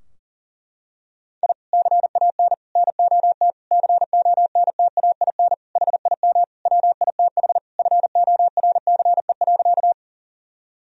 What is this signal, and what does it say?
Morse code, heard as a keyed tone, about 40 wpm, 700 Hz. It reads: I CAN NOT CONTAIN HIM WITH FORCE.
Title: Sentences from Top 1000 Words - Review Part 3 - 40wpm